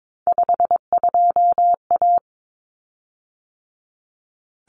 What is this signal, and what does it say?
Morse code, heard as a keyed tone, about 22 wpm, 700 Hz. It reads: 52A
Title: Common Cut Numbers - 3-Digit 22wpm